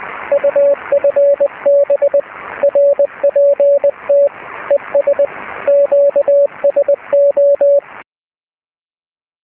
Morse code, heard as a keyed tone, about 20 wpm, 570 Hz, with radio noise. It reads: UFB RPT ES QSO
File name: zs6bte